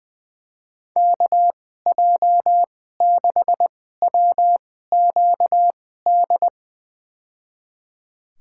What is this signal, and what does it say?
Morse code, heard as a keyed tone, about 20 wpm, spaced as for 14 wpm, 700 Hz. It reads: KJ6WQD
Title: USA Call Signs - 6 Characters - 2x Character Spacing - 20wpm